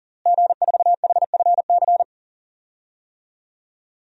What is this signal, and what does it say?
Morse code, heard as a keyed tone, about 40 wpm, 700 Hz. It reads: G4HFC
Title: DXCC-100 Call Signs - 5 Characters - Repeated 3x - 40wpm